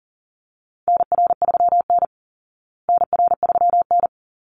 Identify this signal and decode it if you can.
Morse code, heard as a keyed tone, about 40 wpm, 700 Hz. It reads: NR3N NR3N
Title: Call Signs Repeated 3x - 40wpm